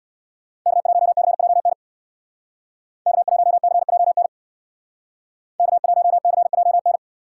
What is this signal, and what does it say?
Morse code, heard as a keyed tone, about 62 wpm, 700 Hz. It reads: K1CJN K1CJN K1CJN
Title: Speed Racing - USA Call Signs 50wpm